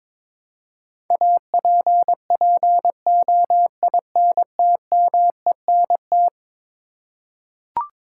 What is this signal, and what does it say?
Morse code, heard as a keyed tone, about 22 wpm, 700 Hz. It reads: APPOINTMENT
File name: practice / 22wpm / Eleven Letter Words - 22wpm